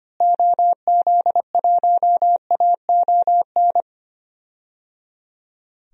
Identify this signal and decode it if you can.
Morse code, heard as a keyed tone, about 25 wpm, 700 Hz. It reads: OZ1AON
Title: DXCC-100 Call Signs - 6 Characters - Repeated 3x - 25wpm